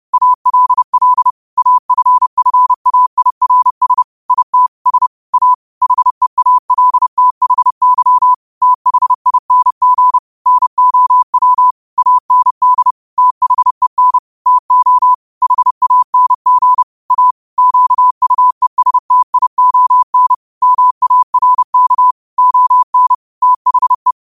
Morse code, heard as a keyed tone, about 30 wpm, 1000 Hz. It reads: ALL AFFAIRS IT S A HEALTHY THING NOW AND THEN TO HANG A QUESTION MARK ON THE
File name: qotdfiles / 30wpm